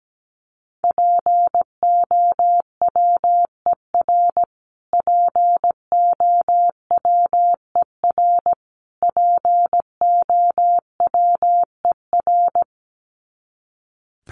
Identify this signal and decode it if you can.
Morse code, heard as a keyed tone, about 17 wpm, 700 Hz. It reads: POWER POWER POWER
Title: Top 500 Words Repeated 17wpm